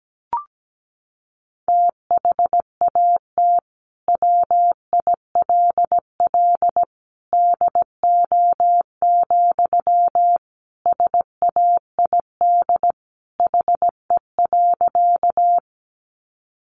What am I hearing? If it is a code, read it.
Morse code, heard as a keyed tone, about 17 wpm, 700 Hz. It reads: THAT WILL DO, SAID HE.
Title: Sentences from Top 100 Words 17wpm